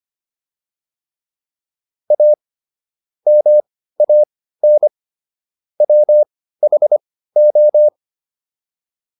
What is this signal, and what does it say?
Morse code, heard as a keyed tone, about 25 wpm, spaced as for 15 wpm, 600 Hz. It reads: A MAN WHO